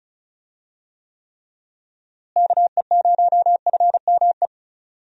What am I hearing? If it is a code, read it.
Morse code, heard as a keyed tone, about 34 wpm, 700 Hz. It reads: KE0FME
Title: USA Call Signs - 6 Characters - Repeated 3x - 35wpm